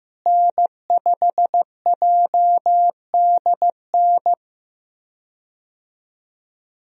Morse code, heard as a keyed tone, about 15 wpm, 700 Hz. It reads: N5JDN
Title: USA Call Signs - 5 Characters - Repeated 3x - 15wpm